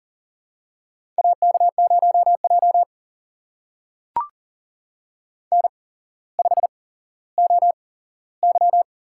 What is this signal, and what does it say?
Morse code, heard as a keyed tone, about 40 wpm, spaced as for 17 wpm, 700 Hz. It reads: AK0J N5OY
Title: USA Call Signs - 4 Characters - 8x Character Spacing - 40wpm